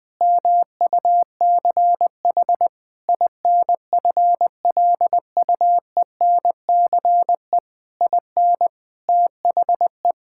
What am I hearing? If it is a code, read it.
Morse code, heard as a keyed tone, about 21 wpm, 700 Hz. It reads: MUCH INFLUENCE IN THE
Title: Sentences from Top 1600 Words 20wpm